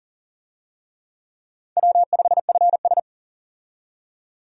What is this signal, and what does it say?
Morse code, heard as a keyed tone, about 38 wpm, 690 Hz. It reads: W5FS
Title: USA Call Signs - 4 Characters - 40wpm